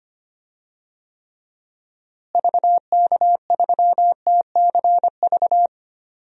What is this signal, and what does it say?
Morse code, heard as a keyed tone, about 25 wpm, 700 Hz. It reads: VK3TCV
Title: DXCC-100 Call Signs - 6 Characters - Repeated 2x - 25wpm